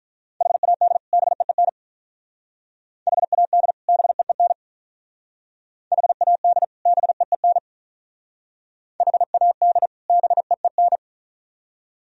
Morse code, heard as a keyed tone, about 42 wpm, 700 Hz. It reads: HAD BEEN HAD BEEN HAD BEEN HAD BEEN
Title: Speed-Racing - Sets of 2 Words 35wpm